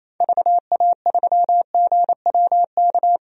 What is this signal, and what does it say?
Morse code, heard as a keyed tone, about 28 wpm, 700 Hz. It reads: VA3GWK
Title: DXCC-100 Call Signs - 6 Characters - 28wpm